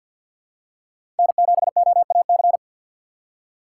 Morse code, heard as a keyed tone, about 47 wpm, 700 Hz. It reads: N7OAC
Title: USA Call Signs - 5 Characters - 4x Character Spacing - 50wpm